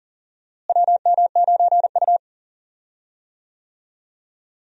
Morse code, heard as a keyed tone, about 39 wpm, 700 Hz. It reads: WM9U
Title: USA Call Signs - 4 Characters - Repeated 2x - 40wpm